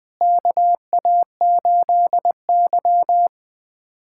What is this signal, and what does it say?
Morse code, heard as a keyed tone, about 20 wpm, 700 Hz. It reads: KA8Y
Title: USA Callsign DE Callsign 20wpm